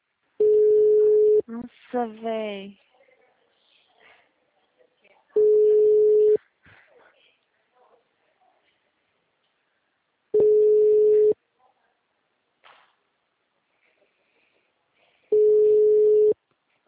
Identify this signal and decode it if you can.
Ringback tone (Europe, 425 Hz), with speech.
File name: out- -2013- - - .81507